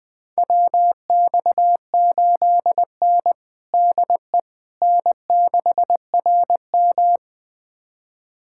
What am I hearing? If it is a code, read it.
Morse code, heard as a keyed tone, about 21 wpm, 700 Hz. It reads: WX8N DE N6RM
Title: USA Callsign DE Callsign - 4 Characters - 20wpm